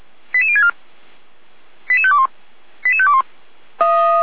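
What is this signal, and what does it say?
Five-tone selective call (ZVEI), heard as quick runs of five tones, tones of about 70 ms. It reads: 88064 88521 88521
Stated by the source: Five-tone ZVEI selective call 88063 received on an SDR.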